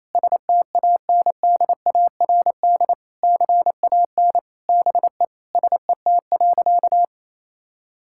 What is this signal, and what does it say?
Morse code, heard as a keyed tone, about 28 wpm, 690 Hz. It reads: STANDARD CAN BE SET.
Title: Sentences from Top 1000 Words - Review Part 6 - 28wpm